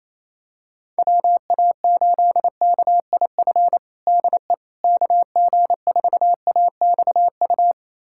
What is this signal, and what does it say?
Morse code, heard as a keyed tone, about 28 wpm, 700 Hz. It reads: WA8KIF DE KG4AXU
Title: USA Callsign DE Callsign - 6 Characters - 28wpm